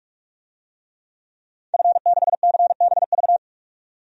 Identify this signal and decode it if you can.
Morse code, heard as a keyed tone, about 46 wpm, 700 Hz. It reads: F6CBV
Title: DXCC-100 Call Signs - 5 Characters - 45wpm